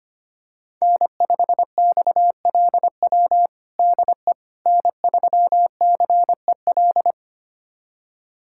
Morse code, heard as a keyed tone, about 25 wpm, 700 Hz. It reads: N5XLW DE N3CEL
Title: USA Callsign DE Callsign 25wpm